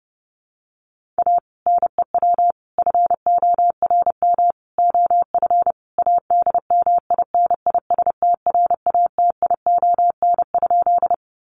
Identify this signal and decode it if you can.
Morse code, heard as a keyed tone, about 28 wpm, 700 Hz. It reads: A NEW FORM OF ADMINISTRATION?
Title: Sentences from Top 1300 Words 30wpm